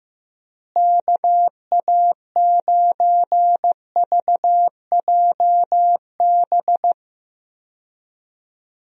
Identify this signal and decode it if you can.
Morse code, heard as a keyed tone, about 15 wpm, 700 Hz. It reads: KA9VJB
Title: USA Call Signs - 6 Characters - Repeated 2x - 15wpm